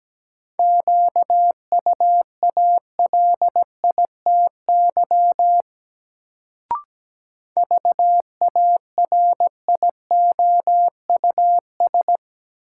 Morse code, heard as a keyed tone, about 17 wpm, 700 Hz. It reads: QUALITY VARIOUS
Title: Sentences from Top 1000 Words - Review Part 4 - New Word Drill - 17wpm